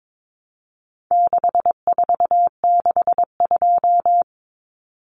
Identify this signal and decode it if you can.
Morse code, heard as a keyed tone, about 22 wpm, 700 Hz. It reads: 6462
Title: Numbers - 4-Digit 22wpm